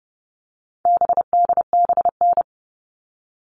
Morse code, heard as a keyed tone, about 30 wpm, 700 Hz. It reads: BDBN
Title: All Cut Numbers - 4-Digit 30wpm